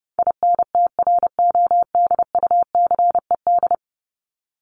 Morse code, heard as a keyed tone, about 30 wpm, 700 Hz. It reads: INTRODUCED
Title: ED Suffix - Top 100 Words with Base Words 30wpm